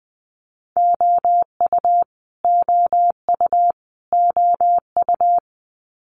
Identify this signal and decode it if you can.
Morse code, heard as a keyed tone, about 20 wpm, 700 Hz. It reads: OU OU OU
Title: Multi-letter Phonemes Repeated 3x 20wpm